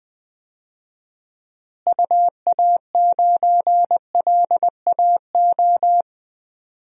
Morse code, heard as a keyed tone, about 21 wpm, 700 Hz. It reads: UA9LAO
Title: DXCC-100 Call Signs - 6 Characters - Repeated 2x - 20wpm